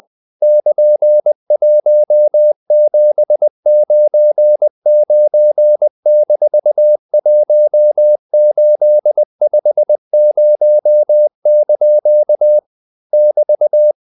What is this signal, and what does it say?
Morse code, heard as a keyed tone, about 20 wpm, 600 Hz. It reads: <KN>1799-1850) <BT>
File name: CW / Practice / EnglishQuotations0320